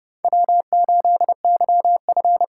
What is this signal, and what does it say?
Morse code, heard as a keyed tone, about 30 wpm, 700 Hz. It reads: W8YF
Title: USA Call Signs - 4 Characters - 3x Character Spacing - 30wpm